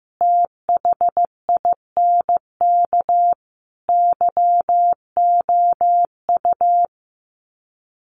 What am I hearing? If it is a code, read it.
Morse code, heard as a keyed tone, about 15 wpm, 700 Hz. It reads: THINK YOU
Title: Sets of 4 Words 15wpm